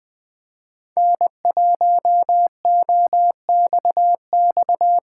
Morse code, heard as a keyed tone, about 21 wpm, 700 Hz. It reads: N1OXX